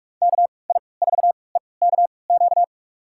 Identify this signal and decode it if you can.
Morse code, heard as a keyed tone, about 41 wpm, spaced as for 25 wpm, 700 Hz. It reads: KI4EKQ
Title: USA Call Signs - 6 Characters - 3x Character Spacing - 45wpm